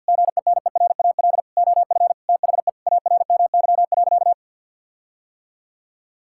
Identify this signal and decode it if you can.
Morse code, heard as a keyed tone, about 50 wpm, 700 Hz. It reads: GENERAL OF THE ARMY.
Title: Sentences from Top 1600 Words 50wpm